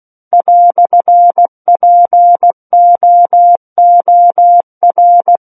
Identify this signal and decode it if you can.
Morse code, heard as a keyed tone, about 16 wpm, 700 Hz. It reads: "POOR
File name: Book One / 07-Chapter 7 - 16 WPM